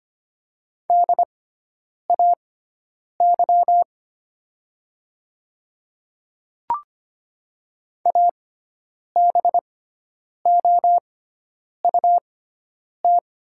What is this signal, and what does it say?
Morse code, heard as a keyed tone, about 25 wpm, spaced as for 9 wpm, 700 Hz. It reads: DAY ABOUT
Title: 6x Character Spacing - Top 100 Words - 25wpm